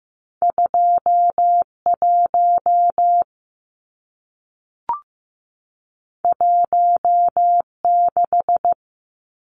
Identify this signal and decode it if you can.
Morse code, heard as a keyed tone, about 15 wpm, 700 Hz. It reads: 2 1 1 6
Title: Numbers - 2-Digit 15wpm